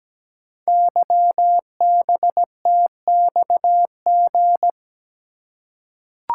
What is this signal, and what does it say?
Morse code, heard as a keyed tone, about 17 wpm, 700 Hz. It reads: YBTXG
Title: Code Groups - 5 Letters - 17wpm